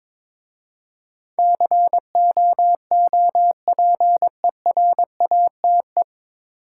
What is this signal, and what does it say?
Morse code, heard as a keyed tone, about 21 wpm, 700 Hz. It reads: COOPERATE
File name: 9 Letter Words - First 3 Letters Given - 22wpm